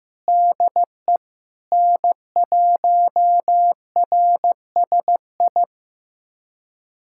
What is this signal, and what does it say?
Morse code, heard as a keyed tone, about 15 wpm, 700 Hz. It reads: DE N1RSI
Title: USA Callsign DE Callsign - 5 Characters - 15wpm